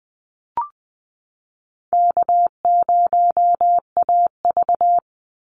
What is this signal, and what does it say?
Morse code, heard as a keyed tone, about 20 wpm, 700 Hz. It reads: K0AV
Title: Easy Call Signs - 20wpm